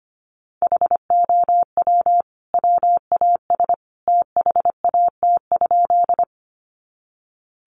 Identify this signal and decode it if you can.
Morse code, heard as a keyed tone, about 25 wpm, 700 Hz. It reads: HOW WAS THAT?
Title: Sentences from Top 100 Words 25wpm